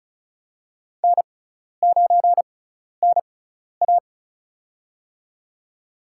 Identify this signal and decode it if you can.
Morse code, heard as a keyed tone, about 34 wpm, spaced as for 12 wpm, 700 Hz. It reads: N9NA
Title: USA Call Signs - 4 Characters - 6x Character Spacing - 35wpm